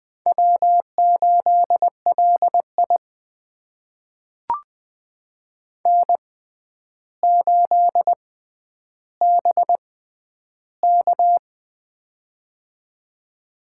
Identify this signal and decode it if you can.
Morse code, heard as a keyed tone, about 21 wpm, spaced as for 10 wpm, 700 Hz. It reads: W8LI N8BK
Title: USA Call Signs - 4 Characters - 6x Character Spacing - 20wpm